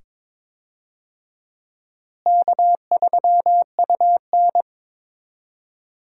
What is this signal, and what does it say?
Morse code, heard as a keyed tone, about 22 wpm, 700 Hz. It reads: K3UN
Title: USA Call Signs - 4 Characters - Repeated 2x - 22wpm